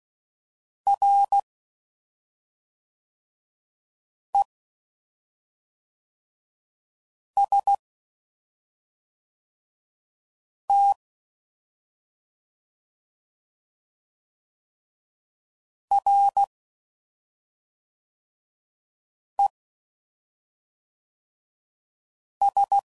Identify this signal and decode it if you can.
Morse code, heard as a keyed tone, about 16 wpm, spaced as for 3 wpm, 800 Hz. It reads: REST RES